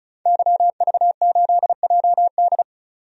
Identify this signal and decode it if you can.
Morse code, heard as a keyed tone, about 34 wpm, 700 Hz. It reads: YV8JD
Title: DXCC-100 Call Signs - 5 Characters - 35wpm